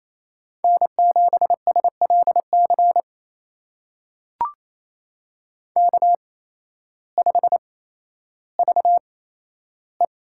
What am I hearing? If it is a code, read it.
Morse code, heard as a keyed tone, about 28 wpm, spaced as for 13 wpm, 700 Hz. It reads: N7SLC K5VE
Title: USA Call Signs - 5 Characters - 8x Character Spacing - 28wpm